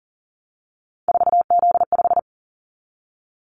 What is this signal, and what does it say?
Morse code, heard as a keyed tone, about 43 wpm, 700 Hz. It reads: 4Z5
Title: Callsign Prefixes for Top 50 DXCC - 40wpm